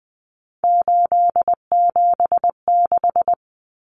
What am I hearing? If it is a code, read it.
Morse code, heard as a keyed tone, about 20 wpm, 700 Hz. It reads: 876
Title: Numbers - 3-Digit 20wpm